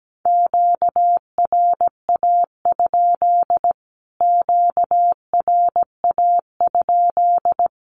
Morse code, heard as a keyed tone, about 17 wpm, 700 Hz. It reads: QRA? QRA?
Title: Q-Codes Repeated 3x 17wpm